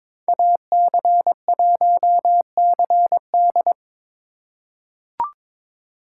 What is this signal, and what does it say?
Morse code, heard as a keyed tone, about 22 wpm, 700 Hz. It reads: AC1CD
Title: USA Call Signs - 5 Characters - 3x Character Spacing - 22wpm